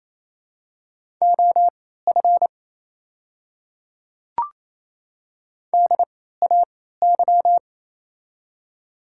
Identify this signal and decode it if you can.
Morse code, heard as a keyed tone, about 28 wpm, spaced as for 16 wpm, 700 Hz. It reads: OF DAY
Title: 3x Character Spacing - Top 100 Words - 28wpm